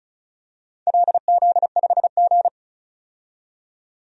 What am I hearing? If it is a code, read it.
Morse code, heard as a keyed tone, about 36 wpm, 700 Hz. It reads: LZ5G
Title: DXCC-100 Call Signs - 4 Characters - 35wpm